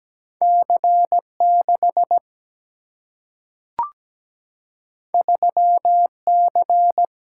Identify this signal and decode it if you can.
Morse code, heard as a keyed tone, about 17 wpm, 700 Hz. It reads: C6 3C
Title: Single-Letter Single-Number 17wpm